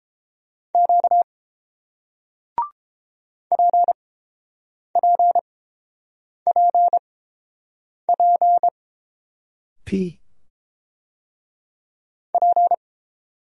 Morse code, heard as a keyed tone, about 30 wpm, spaced as for 7 wpm, 700 Hz. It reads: Q PPPP P